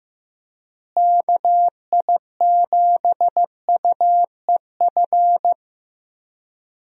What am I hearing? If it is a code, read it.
Morse code, heard as a keyed tone, about 16 wpm, 700 Hz. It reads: KI7UEF
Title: USA Call Signs - 6 Characters - Repeated 3x - 15wpm